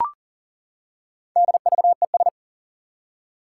Morse code, heard as a keyed tone, about 40 wpm, 700 Hz. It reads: DVES
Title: Four Challenging Letter-Number Combinations 40wpm